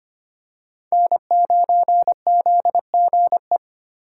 Morse code, heard as a keyed tone, about 25 wpm, 700 Hz. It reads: N9ZGE
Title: USA Call Signs - 5 Characters - 4x Character Spacing - 25wpm